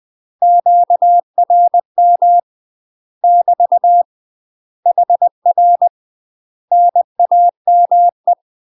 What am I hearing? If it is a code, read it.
Morse code, heard as a keyed tone, about 20 wpm, 700 Hz. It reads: QRM <BT> HR NAME
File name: UK MASTER QSOs TRAINING / 20WPM / UK MASTER QSOs 0068